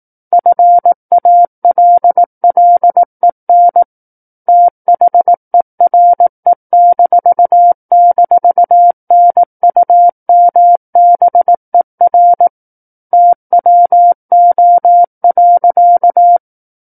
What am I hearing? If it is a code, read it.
Morse code, heard as a keyed tone, about 18 wpm, 700 Hz. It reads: FALLEN THERE--NUMBER TWO.
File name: 18 WPM